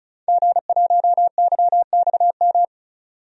Morse code, heard as a keyed tone, about 36 wpm, 700 Hz. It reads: G1YXM
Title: DXCC-100 Call Signs - 5 Characters - 35wpm